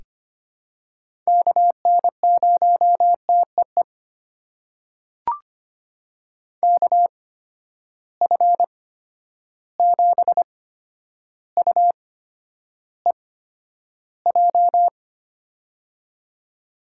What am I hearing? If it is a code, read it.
Morse code, heard as a keyed tone, about 25 wpm, spaced as for 11 wpm, 700 Hz. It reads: KN0TEE KF7UEJ
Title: USA Call Signs - 6 Characters - 8x Character Spacing - 25wpm